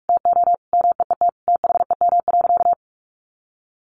Morse code, heard as a keyed tone, about 44 wpm, 700 Hz. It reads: TO MEET THEM.
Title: Sentences from Top 1100 Words 45wpm